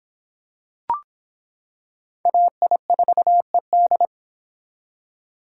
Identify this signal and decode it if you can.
Morse code, heard as a keyed tone, about 26 wpm, 700 Hz. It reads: AI4ED